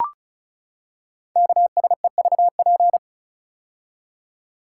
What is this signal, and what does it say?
Morse code, heard as a keyed tone, about 36 wpm, 690 Hz. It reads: KSEVP
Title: Code Groups - 5 Letter:Numbers - 35wpm